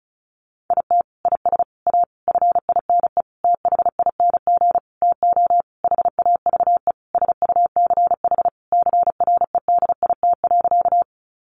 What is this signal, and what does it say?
Morse code, heard as a keyed tone, about 34 wpm, 700 Hz. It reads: IT IS A FINE THING TO HAVE SUCH CREDIT.
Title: Sentences from Top 1000 Words 35wpm